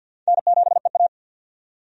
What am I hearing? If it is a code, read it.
Morse code, heard as a keyed tone, about 48 wpm, 700 Hz. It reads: N7EA
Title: USA Call Signs - 4 Characters - 50wpm